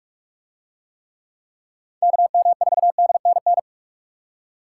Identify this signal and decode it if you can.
Morse code, heard as a keyed tone, about 43 wpm, 700 Hz. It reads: KM4DNN